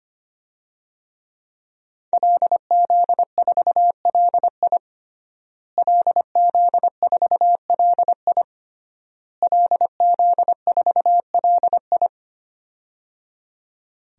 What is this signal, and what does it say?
Morse code, heard as a keyed tone, about 25 wpm, 700 Hz. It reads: LZ4LI LZ4LI LZ4LI
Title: DXCC-100 Call Signs - 5 Characters - Repeated 3x - 25wpm